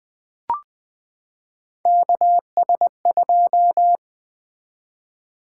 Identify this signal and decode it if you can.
Morse code, heard as a keyed tone, about 20 wpm, 700 Hz. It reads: KS2
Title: Three Letter-Number Combinations - 20wpm